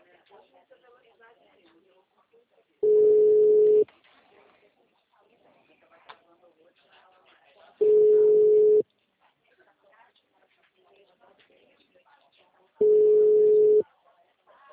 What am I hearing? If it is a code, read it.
Ringback tone (Europe, 425 Hz), with speech.